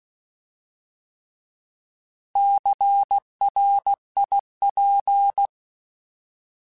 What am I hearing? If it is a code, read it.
Morse code, heard as a keyed tone, about 16 wpm, 800 Hz. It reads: CRIP
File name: Alphabetical / Alphabetical mp3s / Cs @15